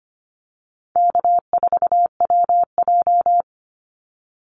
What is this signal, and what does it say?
Morse code, heard as a keyed tone, about 25 wpm, 700 Hz. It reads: K4WJ
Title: Easy Call Signs Repeated 3x - 25wpm